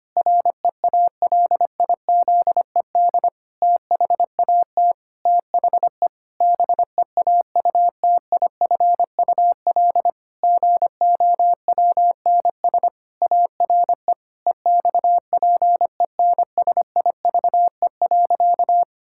Morse code, heard as a keyed tone, about 25 wpm, 700 Hz. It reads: REALIZED THAT THE BEAUTIFUL GOWNS ARE EXPENSIVE.
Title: Pangram Challenge v2 - 25wpm